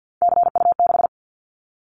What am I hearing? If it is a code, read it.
Morse code, heard as a keyed tone, about 50 wpm, 700 Hz. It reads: CU6
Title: Callsign Prefixes for Top 50 DXCC - 50wpm